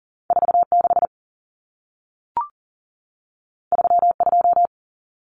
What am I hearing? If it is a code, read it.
Morse code, heard as a keyed tone, about 40 wpm, 700 Hz. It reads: 46 32
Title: Numbers - 2-Digit 40wpm